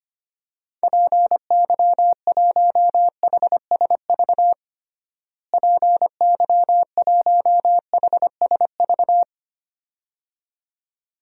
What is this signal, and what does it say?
Morse code, heard as a keyed tone, about 25 wpm, 700 Hz. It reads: PY1HSV PY1HSV
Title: DXCC-100 Call Signs - 6 Characters - Repeated 2x - 25wpm